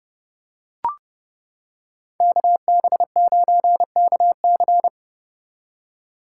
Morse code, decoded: KB9KC

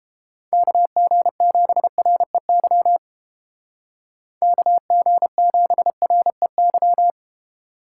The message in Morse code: KG7REY KG7REY

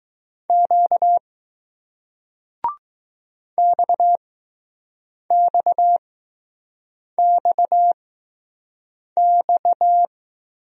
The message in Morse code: Q XXXX

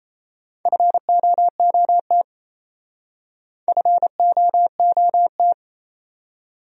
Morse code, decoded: FOOT FOOT

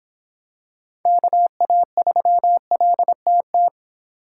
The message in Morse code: KA3LTT

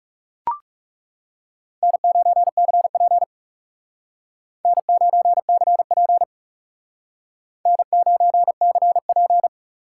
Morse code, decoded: N9CP N9CP N9CP

